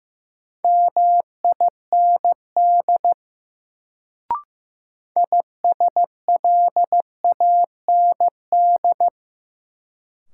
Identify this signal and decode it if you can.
Morse code, heard as a keyed tone, about 15 wpm, 700 Hz. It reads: MIND ISLAND
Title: Sentences from Top 500 Words - New Word Drill - 15wpm